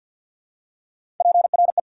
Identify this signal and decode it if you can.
Morse code, heard as a keyed tone, about 48 wpm, 700 Hz. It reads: PRE
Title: Prefixes and Suffixes 50wpm